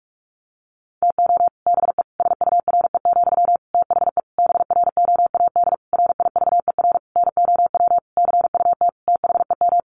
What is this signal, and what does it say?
Morse code, heard as a keyed tone, about 44 wpm, 700 Hz. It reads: TO BE SURE, THE BROAD RIVER NOW CUT THEM
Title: Sentences from Top 1000 Words 45wpm